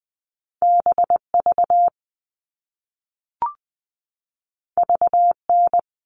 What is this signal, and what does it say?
Morse code, heard as a keyed tone, about 20 wpm, 700 Hz. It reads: B V V N
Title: All Cut Numbers - 2-Digit 20wpm